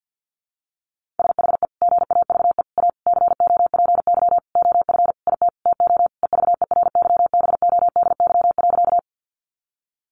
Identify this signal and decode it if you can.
Morse code, heard as a keyed tone, about 50 wpm, 700 Hz. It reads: SHE GAVE A COPY OF IT TO EVERYBODY.
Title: Sentences from Top 1400 Words 50wpm